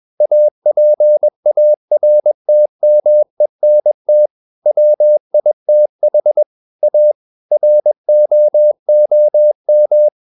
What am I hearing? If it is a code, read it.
Morse code, heard as a keyed tone, about 21 wpm, 600 Hz. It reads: APARTMENT WITH A ROOM
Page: CW med Gnister